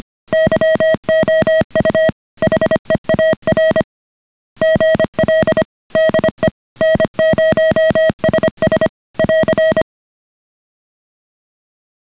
Morse code, heard as a keyed tone, about 25 wpm, 630 Hz. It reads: YOU HEAR GL DE N0SS <AR>